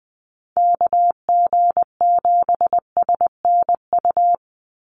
Morse code, heard as a keyed tone, about 20 wpm, 700 Hz. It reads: KG7SNU
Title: USA Call Signs 20wpm